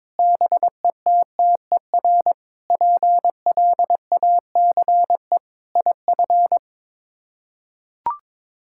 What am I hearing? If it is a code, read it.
Morse code, heard as a keyed tone, about 22 wpm, 700 Hz. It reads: BETTER PLACE IF
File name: practice / 22wpm / Sets of 8 Words - 22wpm